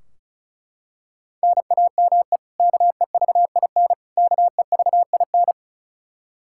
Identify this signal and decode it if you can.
Morse code, heard as a keyed tone, about 34 wpm, 700 Hz. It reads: NAME KEVIN KEVIN
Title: Practice QSO - Exchange 1 - Beginning - 35wpm